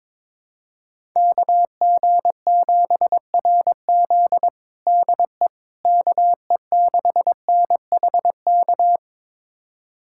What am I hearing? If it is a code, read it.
Morse code, heard as a keyed tone, about 22 wpm, 700 Hz. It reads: KG7RZ DE KE6NHK
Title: USA Callsign DE Callsign 22wpm